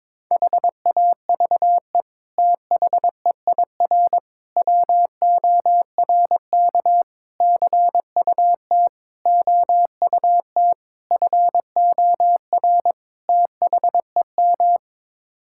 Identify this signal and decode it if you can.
Morse code, heard as a keyed tone, about 22 wpm, 700 Hz. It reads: HAVE THEIR WORK CUT OUT FOR THEM
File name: practice / 22wpm / Sets of 7 Words - 22wpm